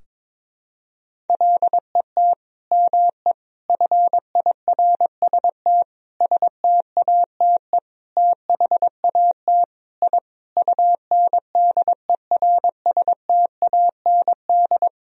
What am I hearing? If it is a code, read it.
Morse code, heard as a keyed tone, about 22 wpm, 700 Hz. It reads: LET ME FIRST STATE THAT I UNDERSTAND
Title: Sentences from Top 1000 Words - Review Part 2 - 22wpm